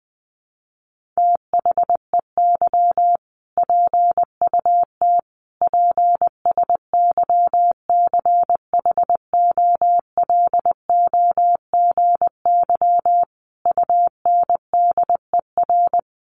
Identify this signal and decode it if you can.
Morse code, heard as a keyed tone, about 20 wpm, 700 Hz. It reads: THEY PUT PSYCHOLOGY UNDER
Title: Sentences from Top 1400 Words 20wpm